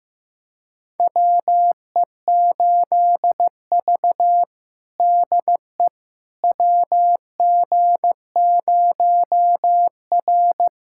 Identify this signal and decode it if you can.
Morse code, heard as a keyed tone, about 15 wpm, 700 Hz. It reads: WE8V DE WG0R
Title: USA Callsign DE Callsign - 4 Characters - 15wpm